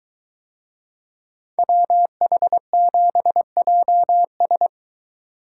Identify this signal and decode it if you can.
Morse code, heard as a keyed tone, about 23 wpm, 700 Hz. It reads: WH7JS